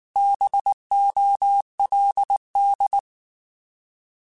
Morse code, heard as a keyed tone, about 19 wpm, 800 Hz. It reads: BOLD
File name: Alphabetical / Alphabetical mp3s / Bs @18